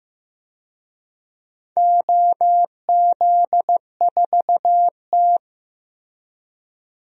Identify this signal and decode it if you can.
Morse code, heard as a keyed tone, about 15 wpm, 700 Hz. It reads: OZ4T
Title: DXCC-100 Call Signs - 4 Characters - Repeated 2x - 15wpm